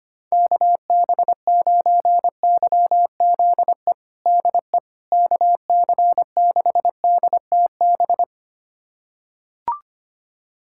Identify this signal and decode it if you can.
Morse code, heard as a keyed tone, about 25 wpm, 700 Hz. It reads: KB9YZE DE KC6DTB
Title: USA Callsign DE Callsign - 6 Characters - 25wpm